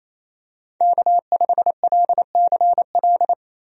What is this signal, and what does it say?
Morse code, heard as a keyed tone, about 28 wpm, 700 Hz. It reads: K5LCL
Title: USA Call Signs - 5 Characters - Repeated 3x - 28wpm